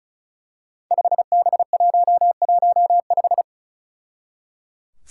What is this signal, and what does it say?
Morse code, heard as a keyed tone, about 36 wpm, 700 Hz. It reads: 5B115
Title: Five Challenging Letter-Number Combinations 35wpm